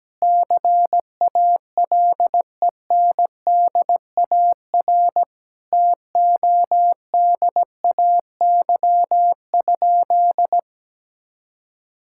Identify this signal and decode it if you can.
Morse code, heard as a keyed tone, about 17 wpm, 700 Hz. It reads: CALENDAR TODAY?
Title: Sentences from Top 1600 Words 17wpm